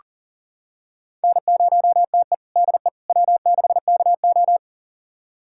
Morse code, heard as a keyed tone, about 40 wpm, 700 Hz. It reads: N0TE DE W6KO
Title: USA Callsign DE Callsign - 4 Characters - 40wpm